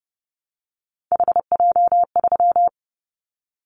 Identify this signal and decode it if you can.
Morse code, heard as a keyed tone, about 30 wpm, 700 Hz. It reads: HJ3
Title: Callsign Prefixes for Top 100 DXCC - 30wpm